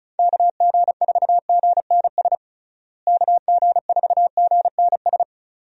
Morse code, heard as a keyed tone, about 34 wpm, 700 Hz. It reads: KG4GNS KG4GNS
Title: USA Call Signs - 6 Characters - Repeated 2x - 35wpm